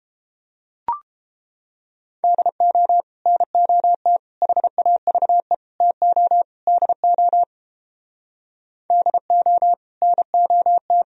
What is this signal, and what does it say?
Morse code, heard as a keyed tone, about 29 wpm, 700 Hz. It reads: DO NOT HAVE TO DO DO NOT